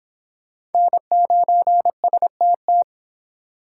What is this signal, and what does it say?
Morse code, heard as a keyed tone, about 32 wpm, 700 Hz. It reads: N9STT